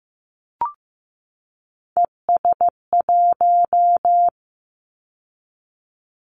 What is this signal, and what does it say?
Morse code, heard as a keyed tone, about 15 wpm, 700 Hz. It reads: ES1
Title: Callsign Prefixes for Top 50 DXCC - 15wpm